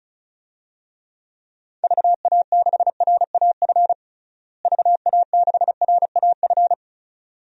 Morse code, heard as a keyed tone, about 36 wpm, 700 Hz. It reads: VA6RAF VA6RAF
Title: DXCC-100 Call Signs - 6 Characters - Repeated 2x - 35wpm